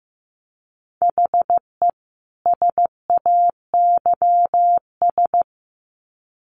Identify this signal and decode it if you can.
Morse code, heard as a keyed tone, about 15 wpm, 700 Hz. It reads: HE SAYS
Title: Sets of 2 Words 15wpm